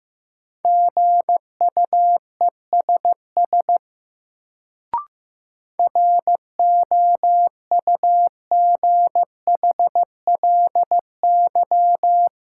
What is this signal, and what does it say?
Morse code, heard as a keyed tone, about 15 wpm, 700 Hz. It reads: GUESS ROUGHLY
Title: Sentences from Top 1500 Words - New Word Drill - 15wpm